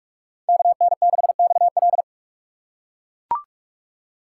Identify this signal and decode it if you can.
Morse code, heard as a keyed tone, about 42 wpm, 700 Hz. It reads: KN6XL
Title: USA Call Signs - 5 Characters - 3x Character Spacing - 45wpm